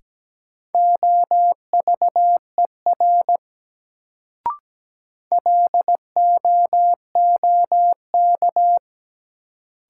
Morse code, heard as a keyed tone, about 17 wpm, 700 Hz. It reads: OVER LOOK
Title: Sentences from Top 100 Words - New Word Drill - 17wpm